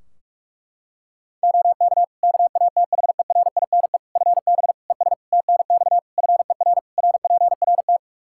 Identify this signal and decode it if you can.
Morse code, heard as a keyed tone, about 41 wpm, 700 Hz. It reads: OK KATHERINE FB ES TNX FER RPRT
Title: K1USN SST QSO Practice 45wpm